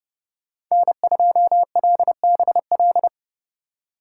Morse code, heard as a keyed tone, about 30 wpm, 700 Hz. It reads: N2LBL
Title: USA Call Signs - 5 Characters - 3x Character Spacing - 30wpm